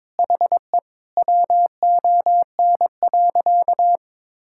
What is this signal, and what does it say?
Morse code, heard as a keyed tone, about 22 wpm, 700 Hz. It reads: HE WON.
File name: Sentences with 7-Word Phrases - 22wpm